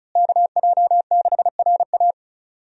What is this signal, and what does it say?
Morse code, heard as a keyed tone, about 34 wpm, 700 Hz. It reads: KJ6RA
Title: USA Call Signs - 5 Characters - Repeated 3x - 35wpm